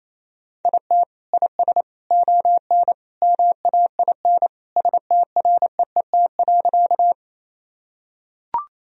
Morse code, heard as a keyed tone, about 27 wpm, 700 Hz. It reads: IT IS ON MAIN STREET.
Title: Sentences from Top 1000 Words - Review Part 2 - 28wpm